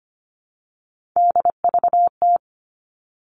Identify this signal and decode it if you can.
Morse code, heard as a keyed tone, about 25 wpm, 700 Hz. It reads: DVT
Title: All Cut Numbers - 3-Digit 25wpm